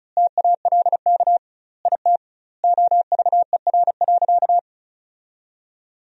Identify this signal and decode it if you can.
Morse code, heard as a keyed tone, about 34 wpm, 700 Hz. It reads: TALK IT OVER.
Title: 2x Word-Spacing - Sentences from Top 500 Words - 35wpm